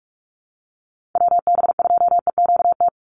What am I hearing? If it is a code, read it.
Morse code, heard as a keyed tone, about 46 wpm, 700 Hz. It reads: WB2EQT
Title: USA Call Signs 45wpm